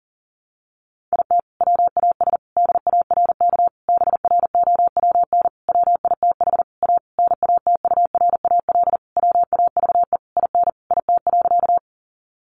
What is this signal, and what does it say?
Morse code, heard as a keyed tone, about 40 wpm, 700 Hz. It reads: IT WAS DARK BROWN WITH A NATURAL WAVE IN IT.
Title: Sentences from Top 1400 Words 40wpm